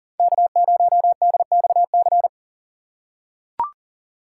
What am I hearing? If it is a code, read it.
Morse code, heard as a keyed tone, about 38 wpm, 700 Hz. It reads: K0DXC
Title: K1USN SST QSO Practice - 40wpm